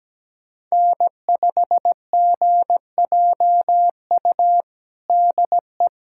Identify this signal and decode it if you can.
Morse code, heard as a keyed tone, about 17 wpm, 700 Hz. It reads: N5GJU DE
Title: USA Callsign DE Callsign - 5 Characters - 17wpm